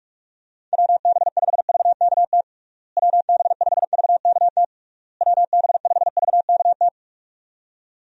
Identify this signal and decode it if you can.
Morse code, heard as a keyed tone, about 46 wpm, 700 Hz. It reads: WB5VKT WB5VKT WB5VKT
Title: USA Call Signs - 6 Characters - Repeated 3x - 45wpm